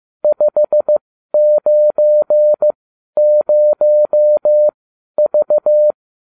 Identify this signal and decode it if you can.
Morse code, heard as a keyed tone, about 15 wpm, spaced as for 11 wpm, 600 Hz. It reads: 590V